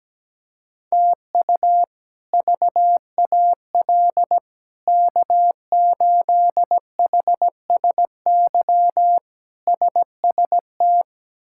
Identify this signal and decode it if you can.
Morse code, heard as a keyed tone, about 17 wpm, 700 Hz. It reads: TU VAL K8HSY SST
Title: K1USN SST QSO Practice - 17wpm